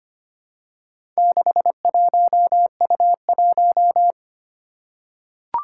Morse code, heard as a keyed tone, about 25 wpm, 700 Hz. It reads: 61U1
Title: Four Challenging Letter-Number Combinations 25wpm